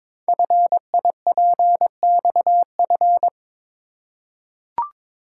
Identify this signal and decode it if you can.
Morse code, heard as a keyed tone, about 22 wpm, 700 Hz. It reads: FIPXF
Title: Code Groups - 5 Letters - 22wpm